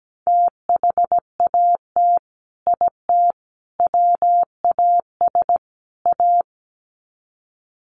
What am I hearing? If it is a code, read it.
Morse code, heard as a keyed tone, about 17 wpm, 700 Hz. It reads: THAT IT WAS A
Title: Sets of 4 Words 17wpm